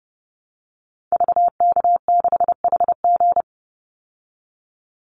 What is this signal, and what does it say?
Morse code, heard as a keyed tone, about 30 wpm, 700 Hz. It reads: VK6HG
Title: Call Signs Repeated 3x - 30wpm